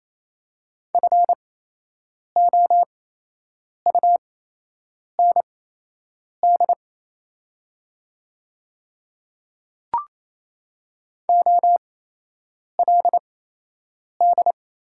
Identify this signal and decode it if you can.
Morse code, heard as a keyed tone, about 28 wpm, spaced as for 8 wpm, 700 Hz. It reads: FOUND OLD
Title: 8x Character Spacing - Top 300 Words - 28wpm